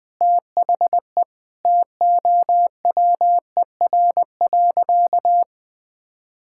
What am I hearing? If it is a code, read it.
Morse code, heard as a keyed tone, about 21 wpm, 700 Hz. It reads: THE TOWER.
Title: Pangram Challenge v2 - 20wpm